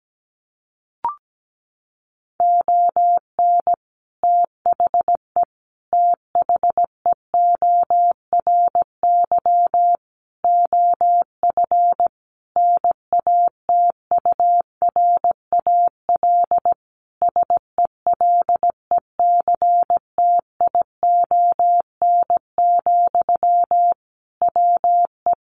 Morse code, heard as a keyed tone, about 17 wpm, 700 Hz. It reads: ON THE THEORY OF NATURAL SELECTION, WE
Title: Sentences from Top 1400 Words 17wpm